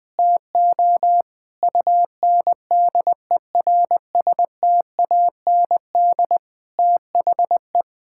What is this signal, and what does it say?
Morse code, heard as a keyed tone, about 20 wpm, 700 Hz. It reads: TO UNDERSTAND THE
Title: Sentences with 7-Word Phrases - Spoken Beginning - 20wpm